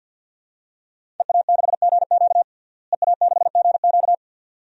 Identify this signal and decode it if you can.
Morse code, heard as a keyed tone, about 51 wpm, 700 Hz. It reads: EA6GQ EA6GQ
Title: DXCC-100 Call Signs - 5 Characters - Repeated 2x - 50wpm